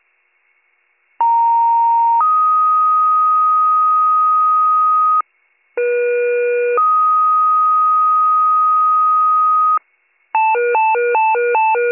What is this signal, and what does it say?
Two-tone page, heard as a one-second tone, then a three-second tone, about 930 then 1250 Hz, about 480 then 1250 Hz. It is heard from 1.2 to 9.8 s.